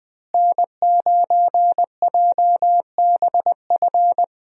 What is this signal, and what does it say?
Morse code, heard as a keyed tone, about 20 wpm, 700 Hz. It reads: N9JBF
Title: USA Call Signs - 5 Characters - Repeated 2x - 20wpm